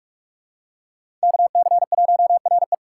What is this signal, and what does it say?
Morse code, heard as a keyed tone, about 47 wpm, 700 Hz. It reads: KC1RE
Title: USA Call Signs - 4x Character Spacing - 45wpm